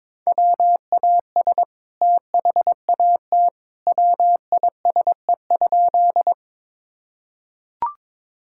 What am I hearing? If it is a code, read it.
Morse code, heard as a keyed tone, about 21 wpm, 700 Hz. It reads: WAS THAT WISE?
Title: Sentences from Top 1600 Words 22wpm